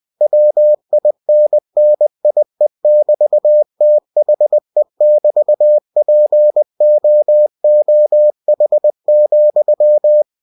Morse code, heard as a keyed tone, about 20 wpm, 600 Hz. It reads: WINNIE<BT>THE<BT>POOH,
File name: cw-books / winnie-the-pooh / winnie-20wpm-page-0025